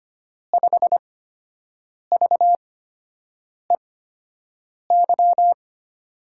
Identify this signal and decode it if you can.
Morse code, heard as a keyed tone, about 25 wpm, 700 Hz. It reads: 5 V E Y
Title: USA Call Signs - 8x Character Spacing - 25wpm